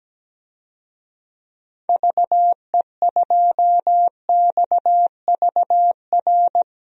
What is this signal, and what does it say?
Morse code, heard as a keyed tone, about 17 wpm, 700 Hz. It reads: VE2XVR
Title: DXCC-100 Call Signs - 6 Characters - 17wpm